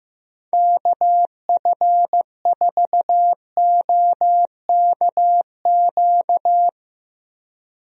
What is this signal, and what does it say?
Morse code, heard as a keyed tone, about 15 wpm, 700 Hz. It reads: KF4OKQ
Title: USA Call Signs - 6 Characters - Repeated 3x - 15wpm